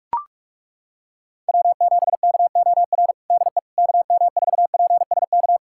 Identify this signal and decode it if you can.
Morse code, heard as a keyed tone, about 42 wpm, 700 Hz. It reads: W7KOR DE KM4PSK
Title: USA Callsign DE Callsign 45wpm